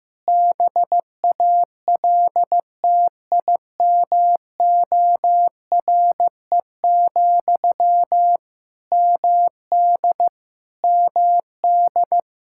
Morse code, heard as a keyed tone, about 15 wpm, 700 Hz. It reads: BALTIMORE, MD MD
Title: Practice QSO - Exchange 1 - Beginning - 15wpm